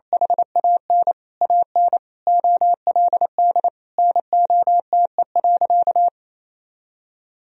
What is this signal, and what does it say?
Morse code, heard as a keyed tone, about 28 wpm, 700 Hz. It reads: HAN AN OLD NOTE.
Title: Sentences from Top 1000 Words - Review Part 2 - 28wpm